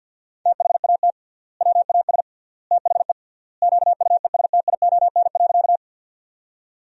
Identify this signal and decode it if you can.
Morse code, heard as a keyed tone, about 50 wpm, 700 Hz. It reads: THAT WAS THE QUESTION.
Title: 3x Word-Spacing - Sentences from Top 400 Words - 50wpm